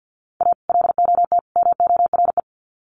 Morse code, heard as a keyed tone, about 50 wpm, 700 Hz. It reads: A LOT MORE
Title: Sets of 3 Words 50wpm